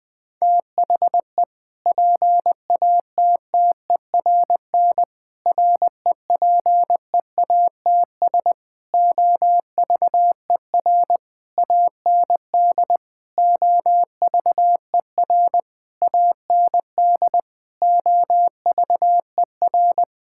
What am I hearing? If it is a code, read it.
Morse code, heard as a keyed tone, about 20 wpm, 700 Hz. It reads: THE PATTERN REPEATS OVER AND OVER AND OVER
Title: Sentences with 6-Word Phrases - 20wpm